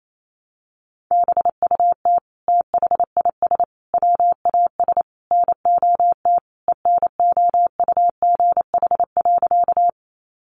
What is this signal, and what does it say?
Morse code, heard as a keyed tone, about 28 wpm, 700 Hz. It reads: BUT THIS WAS NOT ENOUGH.
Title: Sentences from Top 600 Words 28wpm